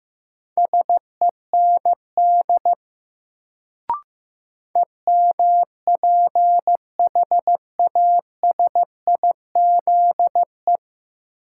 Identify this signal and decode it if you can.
Morse code, heard as a keyed tone, about 15 wpm, 700 Hz. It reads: SEND EMPHASIZE
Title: Sentences from Top 1500 Words - New Word Drill Alt - 15wpm